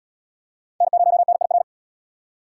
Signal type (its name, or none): Morse code